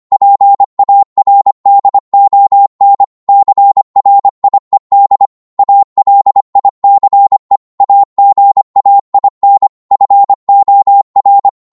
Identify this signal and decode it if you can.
Morse code, heard as a keyed tone, about 25 wpm, 800 Hz. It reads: PARDON CRIED ALICE AGAIN FOR